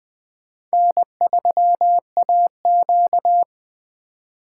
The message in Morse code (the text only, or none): N3AQ